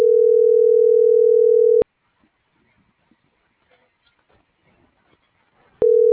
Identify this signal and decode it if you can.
Ringback tone (North America), 440 + 480 Hz.